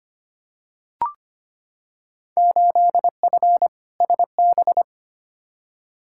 Morse code, heard as a keyed tone, about 25 wpm, 700 Hz. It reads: 8F SB
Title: Field Day Exchanges Encore - 25wpm